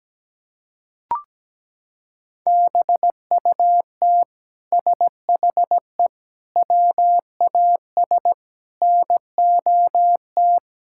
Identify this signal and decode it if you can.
Morse code, heard as a keyed tone, about 17 wpm, 700 Hz. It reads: BUT SHE WAS NOT
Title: Sentences from Top 1000 Words - Review Part 6 - 17wpm